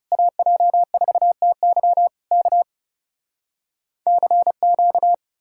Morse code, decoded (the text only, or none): AJ4TY K CQ